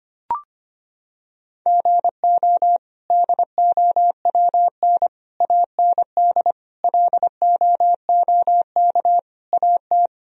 Morse code, heard as a keyed tone, about 25 wpm, 700 Hz. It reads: GO DOWN AND LOOK AT
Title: Sentences from Top 1000 Words - Review Part 1 - 25wpm